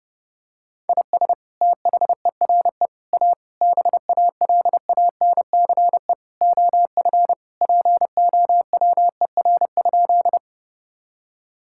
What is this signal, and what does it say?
Morse code, heard as a keyed tone, about 30 wpm, 700 Hz. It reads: IS THERE A BALANCE OF POWER?
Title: Sentences from Top 1000 Words - Review Part 6 - 30wpm